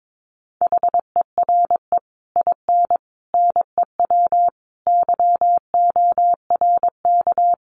Morse code, heard as a keyed tone, about 22 wpm, 700 Hz. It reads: HERE IN NEW YORK
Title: Sets of 5 Words Encore2 22wpm